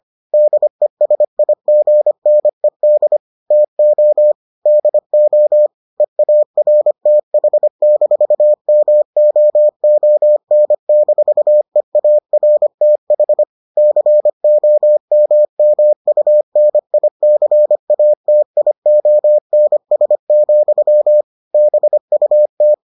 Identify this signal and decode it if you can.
Morse code, heard as a keyed tone, about 25 wpm, 600 Hz. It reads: DESIGNED TO DO EARTH-MOON-EARTH COMMUNICATIONS, BUT